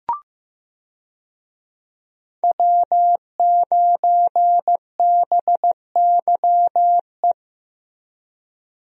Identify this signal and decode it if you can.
Morse code, heard as a keyed tone, about 16 wpm, 700 Hz. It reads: W9BYE
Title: USA Call Signs - 5 Characters - Repeated 3x - 15wpm